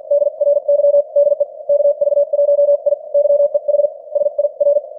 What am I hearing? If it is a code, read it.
Morse code, heard as a keyed tone, about 50 wpm, spaced as for 43 wpm, 590 Hz. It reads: DRY DE KV1I YEH SIR
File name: archived / CWBros